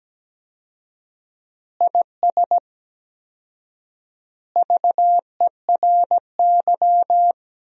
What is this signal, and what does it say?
Morse code, heard as a keyed tone, about 17 wpm, 700 Hz. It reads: IS VERY
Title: 4x Word-Spacing - Sentences from Top 400 Words - 17wpm